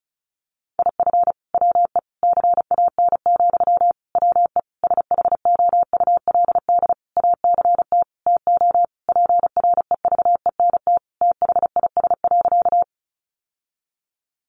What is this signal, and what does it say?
Morse code, heard as a keyed tone, about 34 wpm, 700 Hz. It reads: IF WE CAN, WE SHOULD ACT TO PREVENT THIS.
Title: Sentences from Top 1100 Words 35wpm